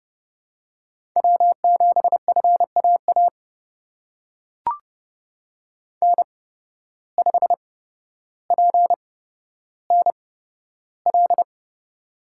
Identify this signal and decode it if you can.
Morse code, heard as a keyed tone, about 30 wpm, spaced as for 13 wpm, 700 Hz. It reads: W7FAA N5PNL
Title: USA Call Signs - 5 Characters - 8x Character Spacing - 30wpm